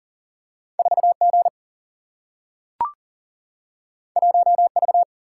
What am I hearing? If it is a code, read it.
Morse code, heard as a keyed tone, about 39 wpm, 700 Hz. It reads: 4G 1V
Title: Single-Letter Single-Number 40wpm